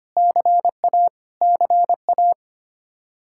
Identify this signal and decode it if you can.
Morse code, heard as a keyed tone, about 25 wpm, 700 Hz. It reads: CA CA
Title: Practice QSO - Exchange 1 - Beginning - 25wpm